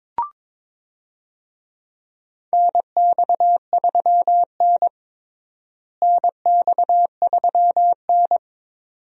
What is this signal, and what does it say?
Morse code, heard as a keyed tone, about 22 wpm, 700 Hz. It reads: NX3N NX3N
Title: USA Call Signs - 4 Characters - Repeated 2x - 22wpm